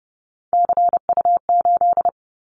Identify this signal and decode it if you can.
Morse code, heard as a keyed tone, about 31 wpm, 700 Hz. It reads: CU8
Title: Callsign Prefixes for Top 50 DXCC - 30wpm